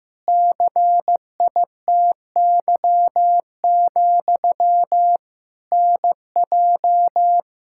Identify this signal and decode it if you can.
Morse code, heard as a keyed tone, about 16 wpm, 700 Hz. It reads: CITY, NJ
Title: Practice QSO - Exchange 1 - Beginning - 15wpm